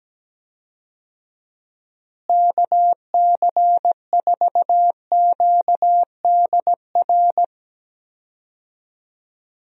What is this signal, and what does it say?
Morse code, heard as a keyed tone, about 17 wpm, 700 Hz. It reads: KC4QDR
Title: USA Call Signs - 6 Characters - Repeated 2x - 17wpm